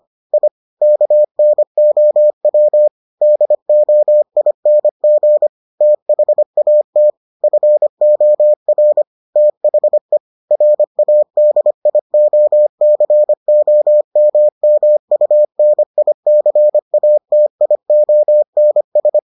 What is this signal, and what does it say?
Morse code, heard as a keyed tone, about 25 wpm, 600 Hz. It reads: I KNOW DOING THAT FOR THE RADIOCOMMUNICATIONS